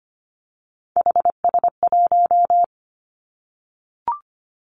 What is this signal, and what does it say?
Morse code, heard as a keyed tone, about 25 wpm, 700 Hz. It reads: HS1
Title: Three Challenging Letter-Number Combinations 25wpm